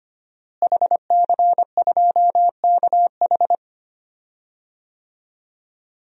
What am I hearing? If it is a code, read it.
Morse code, heard as a keyed tone, about 25 wpm, 700 Hz. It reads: HC2KH
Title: DXCC-100 Call Signs - 5 Characters - Repeated 2x - 25wpm